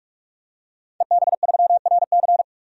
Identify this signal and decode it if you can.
Morse code, heard as a keyed tone, about 47 wpm, 700 Hz. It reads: EB3RC
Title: DXCC-100 Call Signs - 5 Characters - 45wpm